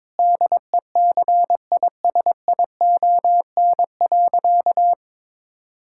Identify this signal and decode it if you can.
Morse code, heard as a keyed tone, about 22 wpm, 700 Hz. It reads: DECISION.
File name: practice / 22wpm / Sentences with 8-Word Phrases - 22wpm